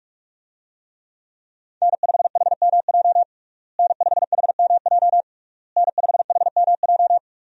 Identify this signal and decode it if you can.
Morse code, heard as a keyed tone, about 46 wpm, 700 Hz. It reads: N5HMJ N5HMJ N5HMJ
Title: USA Call Signs - 5 Characters - Repeated 3x - 45wpm